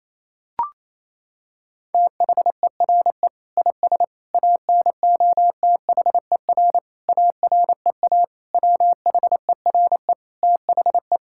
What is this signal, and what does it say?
Morse code, heard as a keyed tone, about 28 wpm, 700 Hz. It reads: THERE IS ANOTHER AREA WHERE THE
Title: Sentences from Top 1000 Words - Review Part 5 - 28wpm